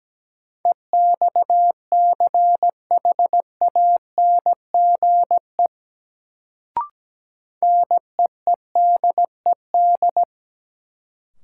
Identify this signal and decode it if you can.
Morse code, heard as a keyed tone, about 17 wpm, 700 Hz. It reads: EXCHANGE NEEDED
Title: Sentences from Top 1000 Words - Review Part 6 - New Word Drill - 17wpm